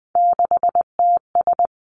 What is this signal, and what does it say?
Morse code, heard as a keyed tone, about 20 wpm, 700 Hz. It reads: 6TS
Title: Three Challenging Letter-Number Combinations 20wpm